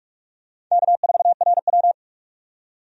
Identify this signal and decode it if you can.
Morse code, heard as a keyed tone, about 45 wpm, 700 Hz. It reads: K4RW
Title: USA Call Signs - 4 Characters - 45wpm